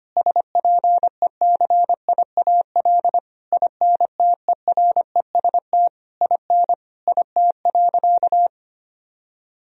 Morse code, heard as a keyed tone, about 25 wpm, 700 Hz. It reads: SPECIAL INTEREST IN IT.
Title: Sentences from Top 1000 Words - Review Part 3 - 25wpm